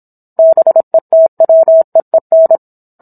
Morse code, heard as a keyed tone, about 27 wpm, 650 Hz. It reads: BETWEEN